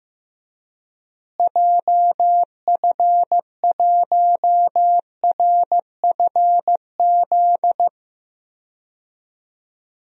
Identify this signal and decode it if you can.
Morse code, heard as a keyed tone, about 15 wpm, 700 Hz. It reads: JF1RFZ
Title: DXCC-100 Call Signs - 6 Characters - Repeated 2x - 15wpm